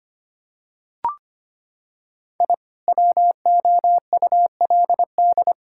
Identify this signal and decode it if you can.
Morse code, heard as a keyed tone, about 25 wpm, 700 Hz. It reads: I WOULD
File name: practice / 25wpm / Sentences with 2-Word Phrases - 25wpm